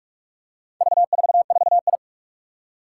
Morse code, heard as a keyed tone, about 44 wpm, 700 Hz. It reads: V44I
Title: Four Challenging Letter-Number Combinations 45wpm